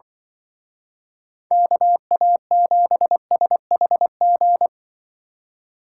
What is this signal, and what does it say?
Morse code, heard as a keyed tone, about 24 wpm, 700 Hz. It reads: KA7SHG
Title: Speed Racing - USA 2x3 Call Signs 20wpm